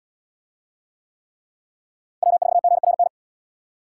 Morse code, heard as a keyed tone, about 75 wpm, 700 Hz. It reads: W4GRN